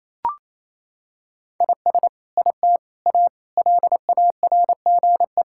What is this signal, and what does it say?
Morse code, heard as a keyed tone, about 28 wpm, 700 Hz. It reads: IS IT A LARGE
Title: Sentences from Top 1000 Words - Review Part 5 - 28wpm